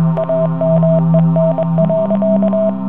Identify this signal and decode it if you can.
Morse code, heard as a keyed tone, about 23 wpm, 650 Hz, with noise. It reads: AMEN.